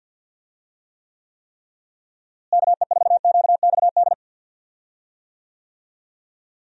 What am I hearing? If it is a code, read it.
Morse code, heard as a keyed tone, about 51 wpm, 700 Hz. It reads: KE4QXD
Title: USA Call Signs - 6 Characters - Repeated 3x - 50wpm